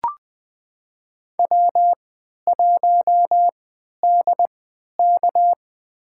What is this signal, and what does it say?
Morse code, heard as a keyed tone, about 20 wpm, spaced as for 11 wpm, 700 Hz. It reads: W1DK
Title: USA Call Signs - 4 Characters - 3x Character Spacing - 20wpm